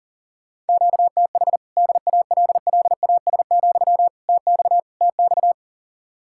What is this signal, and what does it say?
Morse code, heard as a keyed tone, about 48 wpm, 700 Hz. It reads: QTH DALLAS, TX TX
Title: Speed-Racing Common QSO Phrases Encore - 40wpm